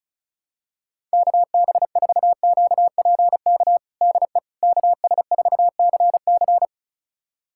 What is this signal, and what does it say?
Morse code, heard as a keyed tone, about 34 wpm, 700 Hz. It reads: KB4QPK DE KS4CC
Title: USA Callsign DE Callsign 35wpm